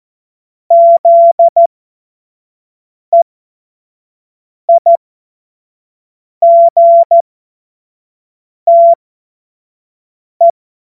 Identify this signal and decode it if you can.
Morse code, heard as a keyed tone, about 14 wpm, spaced as for 5 wpm, 680 Hz. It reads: ZEIGTE
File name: Morsekurs